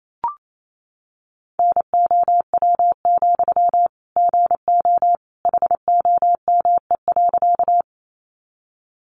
Morse code, heard as a keyed tone, about 28 wpm, 700 Hz. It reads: NOW, GO HOME.
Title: Sentences from Top 200 Words 28wpm